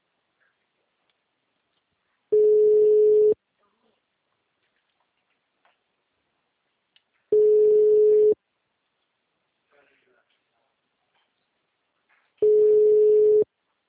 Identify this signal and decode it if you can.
Ringback tone (Europe, 425 Hz), with speech.